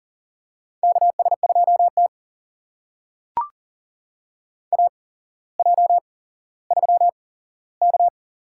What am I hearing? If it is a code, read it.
Morse code, heard as a keyed tone, about 42 wpm, spaced as for 17 wpm, 700 Hz. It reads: KS2T AJ3K
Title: USA Call Signs - 4 Characters - 8x Character Spacing - 40wpm